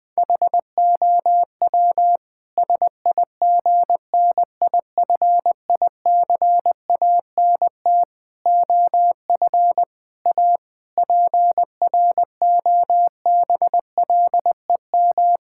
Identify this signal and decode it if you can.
Morse code, heard as a keyed tone, about 19 wpm, 700 Hz. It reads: HOW SIGNIFICANT OF A PROBLEM
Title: Sentences from Top 1000 Words - Review Part 6 - 20wpm